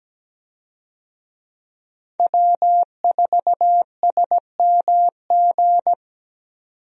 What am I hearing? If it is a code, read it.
Morse code, heard as a keyed tone, about 17 wpm, 700 Hz. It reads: W4SMG
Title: USA Call Signs - 5 Characters - 17wpm